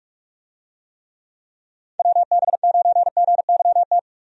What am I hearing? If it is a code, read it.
Morse code, heard as a keyed tone, about 46 wpm, 700 Hz. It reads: WB9GYT
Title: USA Call Signs - 6 Characters - 45wpm